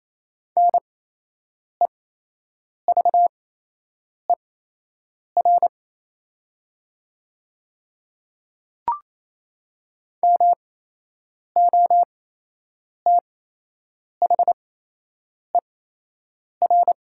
Morse code, heard as a keyed tone, about 28 wpm, spaced as for 8 wpm, 700 Hz. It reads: NEVER MOTHER
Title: 8x Character Spacing - Top 500 Words - 28wpm